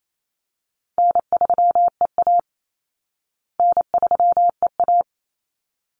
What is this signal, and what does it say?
Morse code, heard as a keyed tone, about 28 wpm, 700 Hz. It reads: N3EA N3EA
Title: Easy Call Signs Repeated 2x - 28wpm